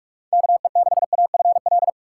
Morse code, heard as a keyed tone, about 46 wpm, 700 Hz. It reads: KE6AFL
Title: USA Call Signs - 6 Characters - 3x Character Spacing - 45wpm